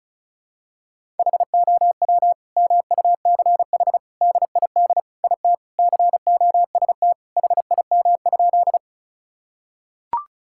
Morse code, heard as a keyed tone, about 34 wpm, 690 Hz. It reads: HOW MUCH DID IT COST HIM?
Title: Sentences from Top 1000 Words - Review Part 5 - 35wpm